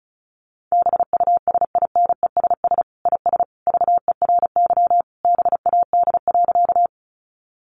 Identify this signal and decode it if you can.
Morse code, heard as a keyed tone, about 34 wpm, 700 Hz. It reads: BUSINESS IS VERY BAD.
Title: Sentences from Top 700 Words 35wpm